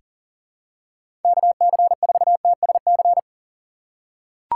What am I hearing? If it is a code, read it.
Morse code, heard as a keyed tone, about 38 wpm, 700 Hz. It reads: KC4TSC
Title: USA Call Signs - 6 Characters - 4x Character Spacing - 40wpm